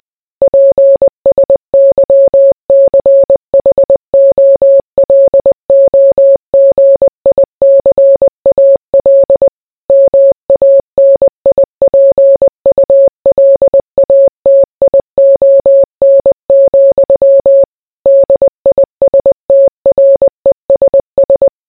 Morse code, heard as a keyed tone, about 20 wpm, 550 Hz. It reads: PSYCHOLOGICAL MANIPULATION, DISTRESS